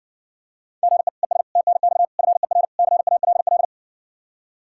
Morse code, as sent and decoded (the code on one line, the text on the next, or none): --. / . ... / - -. -..- / ..-. .-. / -.-. .- .-.. .-..
GE ES TNX FER CALL